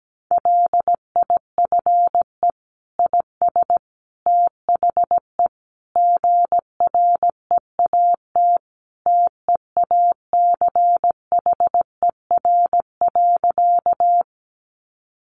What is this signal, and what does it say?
Morse code, heard as a keyed tone, about 17 wpm, 700 Hz. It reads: LIFE IS THE GREAT TEACHER.
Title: Sentences from Top 900 Words 17wpm